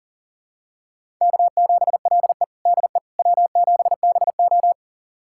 Morse code, heard as a keyed tone, about 37 wpm, 700 Hz. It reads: K7LE DE W7BO
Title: USA Callsign DE Callsign - 4 Characters - 40wpm